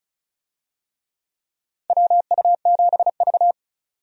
Morse code, heard as a keyed tone, about 34 wpm, 700 Hz. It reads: WU7V
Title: USA Call Signs - 4 Characters - 35wpm